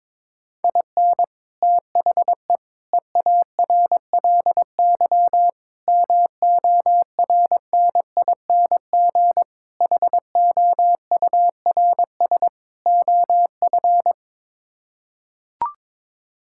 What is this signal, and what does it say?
Morse code, heard as a keyed tone, about 22 wpm, 700 Hz. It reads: IN THE EARLY MORNING HOURS OF
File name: practice / 22wpm / Sets of 6 Words - 22wpm